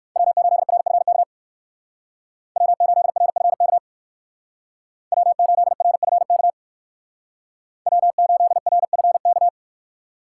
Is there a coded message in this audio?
Yes: Morse code from 0.2 to 9.5 s, heard as a keyed tone, about 58 wpm, 700 Hz. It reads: W8RFK W8RFK W8RFK W8RFK